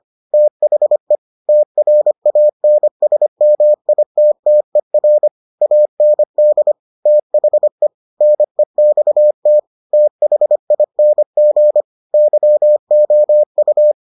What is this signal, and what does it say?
Morse code, heard as a keyed tone, about 25 wpm, 600 Hz. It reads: THE TRANSMITTER AND THE NEXT THING YOU